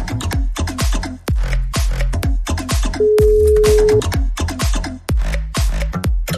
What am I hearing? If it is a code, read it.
Ringback tone (Europe, 425 Hz).